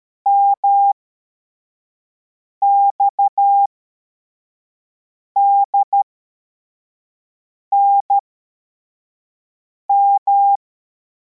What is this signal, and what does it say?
Morse code, heard as a keyed tone, about 13 wpm, spaced as for 4 wpm, 800 Hz. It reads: MXDNM
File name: formation / cw / Text23